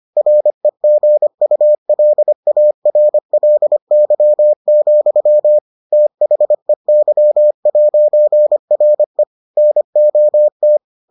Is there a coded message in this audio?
Yes: Morse code from 0.2 to 10.8 s, heard as a keyed tone, about 25 wpm, 600 Hz. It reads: REGULARLY, THEY'RE NOT